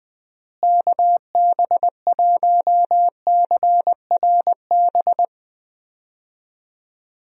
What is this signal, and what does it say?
Morse code, heard as a keyed tone, about 20 wpm, 700 Hz. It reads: KB1CRB
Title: USA Call Signs - 6 Characters - Repeated 2x - 20wpm